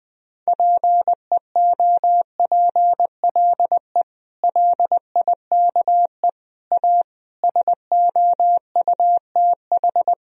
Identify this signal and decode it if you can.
Morse code, heard as a keyed tone, about 20 wpm, 700 Hz. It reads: PEOPLE LIKE A SOUTH
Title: Sentences from Top 1000 Words - Review Part 3 - 20wpm